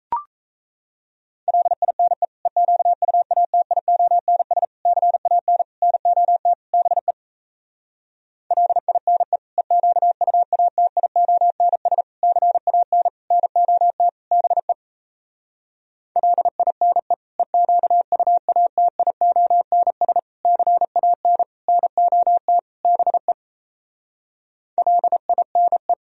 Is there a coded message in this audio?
Yes: Morse code from 1.5 to 25.9 s, heard as a keyed tone, about 38 wpm, 700 Hz. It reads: LINE EQUATIONS CAN NOT BE LINE EQUATIONS CAN NOT BE LINE EQUATIONS CAN NOT BE LINE